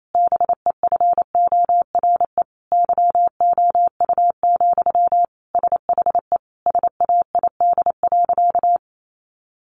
Morse code, heard as a keyed tone, about 28 wpm, 700 Hz. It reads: BEFORE YOU, SHE SAID.
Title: Sentences from Top 400 Words 28wpm